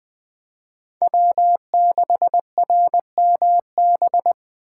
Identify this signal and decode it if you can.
Morse code, heard as a keyed tone, about 20 wpm, 700 Hz. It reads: W6RMB
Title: Speed Racing - USA 1x3 Call Signs 17wpm